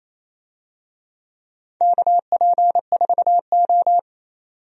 Morse code, heard as a keyed tone, about 27 wpm, 700 Hz. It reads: KP4O
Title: DXCC-100 Call Signs - 4 Characters - 28wpm